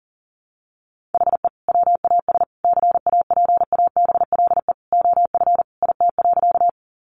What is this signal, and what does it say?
Morse code, heard as a keyed tone, about 40 wpm, 700 Hz. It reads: HE WAS CAPABLE OF IT.
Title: Sentences from Top 1200 Words 40wpm